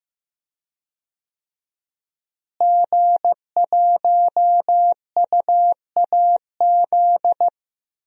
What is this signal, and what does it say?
Morse code, heard as a keyed tone, about 15 wpm, 700 Hz. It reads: G1UAZ
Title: DXCC-100 Call Signs - 5 Characters - Repeated 2x - 15wpm